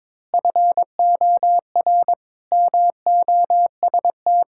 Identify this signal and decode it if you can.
Morse code, heard as a keyed tone, about 22 wpm, 700 Hz. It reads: FOR MOST
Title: Pangram Challenge v2 - 22wpm